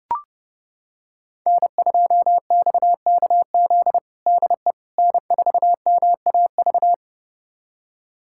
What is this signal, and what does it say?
Morse code, heard as a keyed tone, about 30 wpm, 700 Hz. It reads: N2XKZ DE N4MAV
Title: USA Callsign DE Callsign - 5 Characters - 30wpm